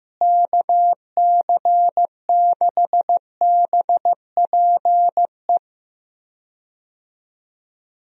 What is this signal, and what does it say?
Morse code, heard as a keyed tone, about 15 wpm, 700 Hz. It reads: KC6BPE
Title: USA Call Signs - 6 Characters - Repeated 2x - 15wpm